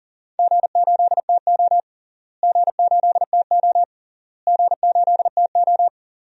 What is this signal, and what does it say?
Morse code, heard as a keyed tone, about 40 wpm, 700 Hz. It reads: G8TO G8TO G8TO
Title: DXCC-100 Call Signs - 4 Characters - Repeated 3x - 40wpm